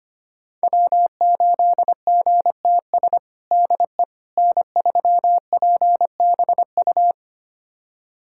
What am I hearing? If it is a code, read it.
Morse code, heard as a keyed tone, about 25 wpm, 700 Hz. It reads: W8GTS DE N3PBU
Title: USA Callsign DE Callsign - 5 Characters - 25wpm